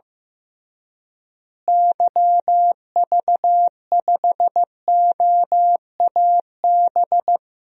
Morse code, heard as a keyed tone, about 15 wpm, 700 Hz. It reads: YV5OAB
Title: DXCC-100 Call Signs - 6 Characters - Repeated 3x - 15wpm